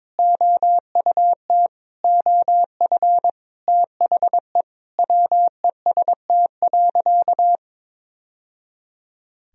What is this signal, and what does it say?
Morse code, heard as a keyed tone, about 22 wpm, 700 Hz. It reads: OUT OF THE WEST.
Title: Sentences from Top 1000 Words - Review Part 2 - 22wpm